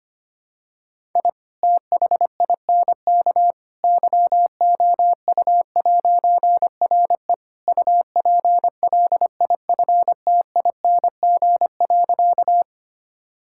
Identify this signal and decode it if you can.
Morse code, heard as a keyed tone, about 25 wpm, 700 Hz. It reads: I THINK YOU'RE UPLIFTING.
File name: practice / 25wpm / All-Age Appropriate Jokes - 25wpm